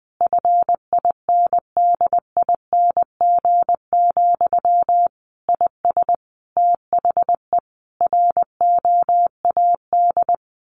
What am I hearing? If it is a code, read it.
Morse code, heard as a keyed tone, about 20 wpm, 700 Hz. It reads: FINDING, IS THE ROAD
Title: Sentences from Top 1100 Words 20wpm